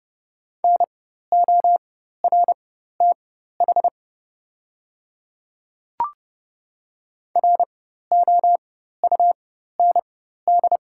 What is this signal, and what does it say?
Morse code, heard as a keyed tone, about 29 wpm, spaced as for 14 wpm, 690 Hz. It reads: NORTH ROUND
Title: 4x Character Spacing - Top 500 Words - 30wpm